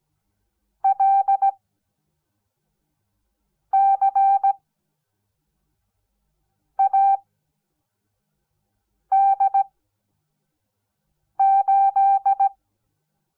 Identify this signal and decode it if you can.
Morse code, heard as a keyed tone, about 17 wpm, spaced as for 4 wpm, 780 Hz. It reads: LCAD8
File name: Onlinelehrgang / morsen / DJ4UF Morsen06